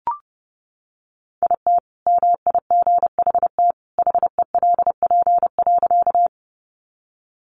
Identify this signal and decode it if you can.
Morse code, heard as a keyed tone, about 30 wpm, 700 Hz. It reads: IT MIGHT HELP.
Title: Sentences from Top 300 Words 30wpm